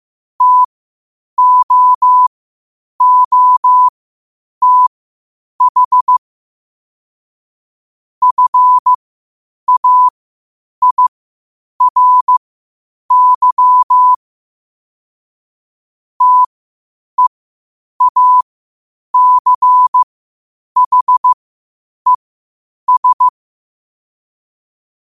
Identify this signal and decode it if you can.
Morse code, heard as a keyed tone, about 15 wpm, spaced as for 8 wpm, 1000 Hz. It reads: TOOTH FAIRY TEACHES
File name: qotdfiles / 8wpm